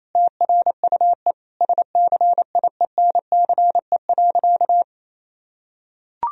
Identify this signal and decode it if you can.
Morse code, heard as a keyed tone, about 28 wpm, 700 Hz. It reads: TRUE SCIENCE.
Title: Sentences from Top 1000 Words - Review Part 4 - 28wpm